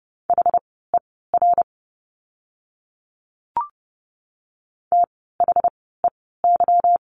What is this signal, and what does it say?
Morse code, heard as a keyed tone, about 30 wpm, spaced as for 17 wpm, 700 Hz. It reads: HER THEY
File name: course / 30:10wpm / 237 Lesson 38f - Letter q - Words 30wpm